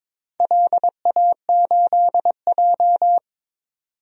Morse code, heard as a keyed tone, about 22 wpm, 700 Hz. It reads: LA8J
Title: DXCC-100 Call Signs - 4 Characters - 22wpm